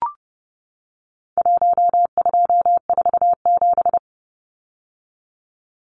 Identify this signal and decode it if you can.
Morse code, heard as a keyed tone, about 29 wpm, 700 Hz. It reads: 1247
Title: Numbers - 4-Digit 30wpm